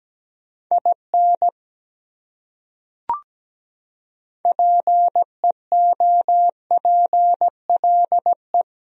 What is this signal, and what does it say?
Morse code, heard as a keyed tone, about 17 wpm, 700 Hz. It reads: IN PEOPLE
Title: Top 100 Words Repeated 17wpm